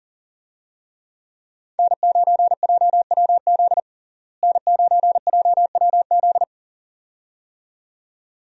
Morse code, heard as a keyed tone, about 40 wpm, 700 Hz. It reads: N9JWZ N9JWZ
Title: USA Call Signs - 5 Characters - Repeated 2x - 40wpm